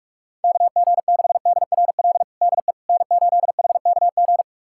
Morse code, heard as a keyed tone, about 46 wpm, 700 Hz. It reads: KG6DRL DE N8HKG
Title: USA Callsign DE Callsign 45wpm